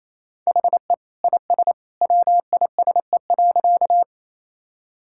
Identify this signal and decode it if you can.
Morse code, heard as a keyed tone, about 28 wpm, 700 Hz. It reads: HE IS WISE.
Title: Sentences from Top 1600 Words 28wpm